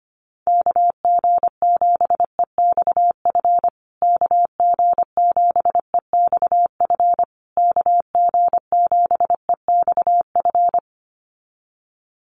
Morse code, heard as a keyed tone, about 25 wpm, 700 Hz. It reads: KG7EXF KG7EXF KG7EXF
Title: USA Call Signs Repeated 3x 25wpm